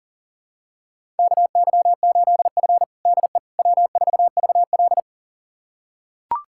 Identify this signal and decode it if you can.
Morse code, heard as a keyed tone, about 40 wpm, 700 Hz. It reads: KY8F DE W4VL
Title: USA Callsign DE Callsign - 4 Characters - 40wpm